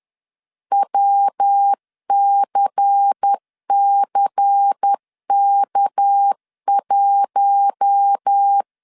Morse code, heard as a keyed tone, about 10 wpm, 800 Hz. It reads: WCCK1